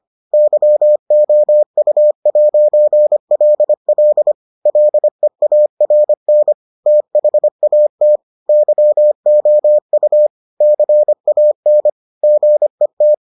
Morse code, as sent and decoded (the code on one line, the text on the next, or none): -.-- --- ..- .----. .-.. .-.. / .-.. .- .-. -. / - .... .- - / -.-- --- ..- / -.-. .- -. / --. -
YOU'LL LEARN THAT YOU CAN GET